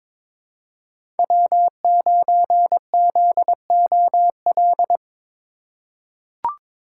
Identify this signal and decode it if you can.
Morse code, heard as a keyed tone, about 22 wpm, 700 Hz. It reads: W9ZOL
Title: USA Call Signs - 5 Characters - 2x Character Spacing - 22wpm